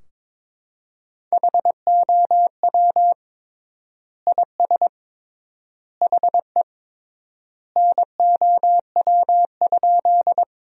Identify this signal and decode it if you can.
Morse code, heard as a keyed tone, about 23 wpm, 700 Hz. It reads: HOW IS HE NOW?
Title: 3x Word-Spacing - Sentences from Top 100 Words - 22wpm